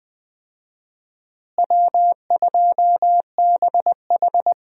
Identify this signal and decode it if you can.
Morse code, heard as a keyed tone, about 20 wpm, 700 Hz. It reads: W2BH
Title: USA Call Signs - 4 Characters - Repeated 3x - 20wpm